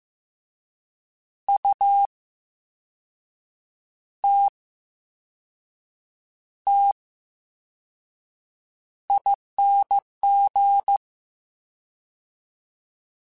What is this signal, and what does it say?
Morse code, heard as a keyed tone, about 15 wpm, spaced as for 5 wpm, 800 Hz. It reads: UTTING
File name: Special INGs / Special INGs-mp3s / Slow ing @13